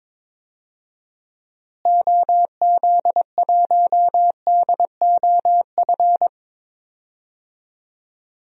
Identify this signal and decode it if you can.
Morse code, heard as a keyed tone, about 22 wpm, 700 Hz. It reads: OZ1DOF
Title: DXCC-100 Call Signs - 6 Characters - Repeated 2x - 22wpm